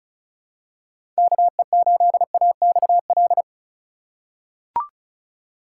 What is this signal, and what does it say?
Morse code, heard as a keyed tone, about 36 wpm, 700 Hz. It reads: KE8AXL
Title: USA Call Signs - 6 Characters - 3x Character Spacing - 35wpm